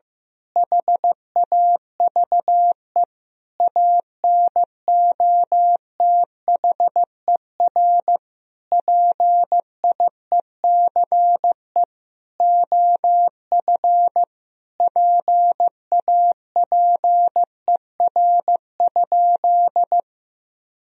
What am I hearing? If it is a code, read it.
Morse code, heard as a keyed tone, about 15 wpm, 700 Hz. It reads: HAVE ANOTHER PIECE OF PAPER?
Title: Sentences from Top 1000 Words - Review Part 2 - 15wpm